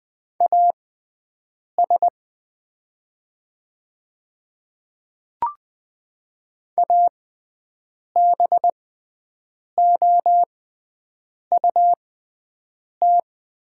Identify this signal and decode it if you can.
Morse code, heard as a keyed tone, about 20 wpm, spaced as for 7 wpm, 700 Hz. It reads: AS ABOUT
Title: 6x Character Spacing - Top 100 Words - 20wpm